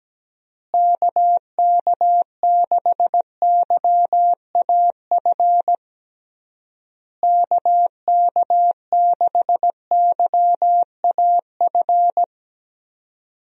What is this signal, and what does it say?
Morse code, heard as a keyed tone, about 17 wpm, 700 Hz. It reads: KK6YAF KK6YAF
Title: USA Call Signs - 6 Characters - Repeated 3x - 17wpm